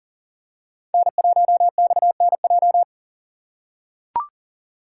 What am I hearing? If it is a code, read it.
Morse code, heard as a keyed tone, about 40 wpm, 700 Hz. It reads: N1XNJ
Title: USA Call Signs - 5 Characters - 6x Character Spacing - 40wpm